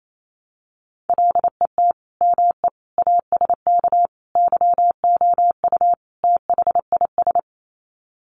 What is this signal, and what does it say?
Morse code, heard as a keyed tone, about 28 wpm, 700 Hz. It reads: LET ME ASK YOU THIS
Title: Sets of 5 Words 28wpm